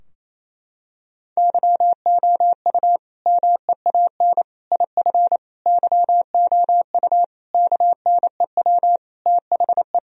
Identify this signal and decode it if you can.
Morse code, heard as a keyed tone, about 28 wpm, 700 Hz. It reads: YOU MEAN IF YOU KNEW THE
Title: Sentences from Top 1000 Words - Review Part 4 - 28wpm